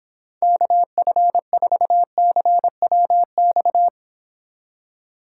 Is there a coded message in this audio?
Yes: Morse code from 0.4 to 3.9 s, heard as a keyed tone, about 26 wpm, 700 Hz. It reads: KF4CWX